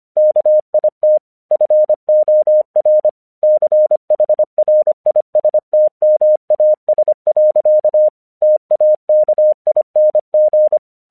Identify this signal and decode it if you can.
Morse code, heard as a keyed tone, about 25 wpm, 610 Hz. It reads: KIT FOR CHRISTMAS. TAKING
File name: 1 English 25wpm / AQC17